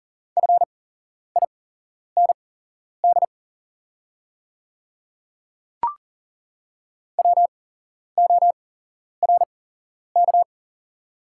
Morse code, heard as a keyed tone, about 42 wpm, spaced as for 11 wpm, 690 Hz. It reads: FIND WORK
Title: 8x Character Spacing - Top 200 Words - 40wpm